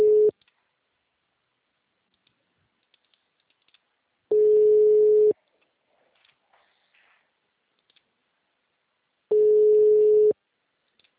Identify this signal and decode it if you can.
Ringback tone (Europe, 425 Hz).